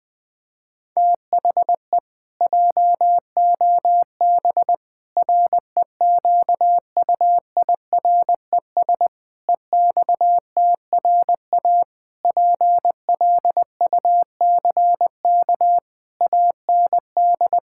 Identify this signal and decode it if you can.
Morse code, heard as a keyed tone, about 20 wpm, 700 Hz. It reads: THE JOB REQUIRES EXTRA PLUCK AND
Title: Pangram Challenge v2 - 20wpm